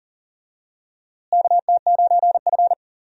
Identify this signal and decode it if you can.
Morse code, heard as a keyed tone, about 40 wpm, 700 Hz. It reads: KT9F
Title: USA Call Signs - 4 Characters - Repeated 2x - 40wpm